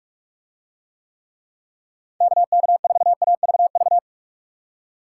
Morse code, heard as a keyed tone, about 44 wpm, 700 Hz. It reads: KK4AVV